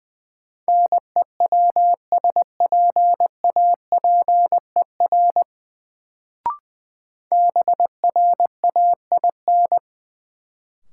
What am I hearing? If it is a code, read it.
Morse code, heard as a keyed tone, about 20 wpm, 700 Hz. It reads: NEWSPAPER BRAIN
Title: Sentences from Top 1100 Words - New Word Drill Alt - 20wpm